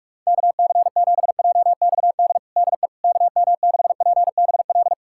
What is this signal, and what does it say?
Morse code, heard as a keyed tone, about 45 wpm, 700 Hz. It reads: KC7JXD DE KM6PBL
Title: USA Callsign DE Callsign - 6 Characters - 45wpm